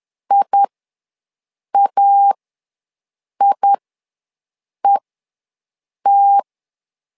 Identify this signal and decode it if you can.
Morse code, heard as a keyed tone, about 11 wpm, spaced as for 6 wpm, 800 Hz. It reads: IAIET
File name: LEZIONI / 2 UHDN / 2 U ws 10wpm 2017